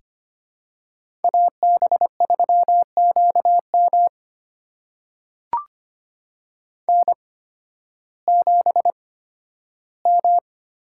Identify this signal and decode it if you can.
Morse code, heard as a keyed tone, about 25 wpm, spaced as for 13 wpm, 700 Hz. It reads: AB3QM N7M
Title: USA Call Signs - 8x Character Spacing - 25wpm